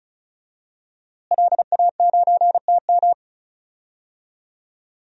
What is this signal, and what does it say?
Morse code, heard as a keyed tone, about 36 wpm, 700 Hz. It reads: LA9TM
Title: DXCC-100 Call Signs - 5 Characters - Repeated 2x - 35wpm